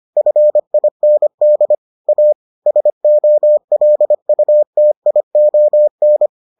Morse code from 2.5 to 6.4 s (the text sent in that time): SOLUTION